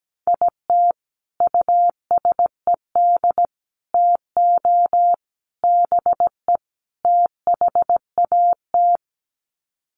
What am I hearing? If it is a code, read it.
Morse code, heard as a keyed tone, about 17 wpm, 700 Hz. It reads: IT USED TO BE THAT
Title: Sets of 5 Words Encore2 17wpm